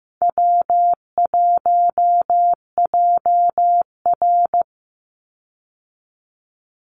Morse code, heard as a keyed tone, about 15 wpm, 700 Hz. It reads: W1JR
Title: Easy Call Signs Repeated 2x - 15wpm